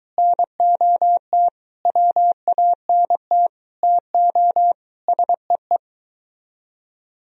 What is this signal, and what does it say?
Morse code, heard as a keyed tone, about 23 wpm, 700 Hz. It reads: NOT WANT TO SEE